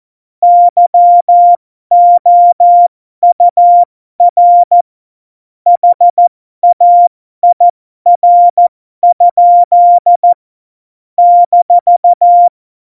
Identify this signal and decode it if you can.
Morse code, heard as a keyed tone, about 14 wpm, spaced as for 12 wpm, 700 Hz. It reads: YOUR HAIR? -
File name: Morsekurs / Witze